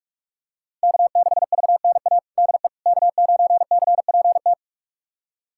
Morse code, heard as a keyed tone, about 47 wpm, 700 Hz. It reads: K6VNA DE K9CPT
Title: USA Callsign DE Callsign - 5 Characters - 45wpm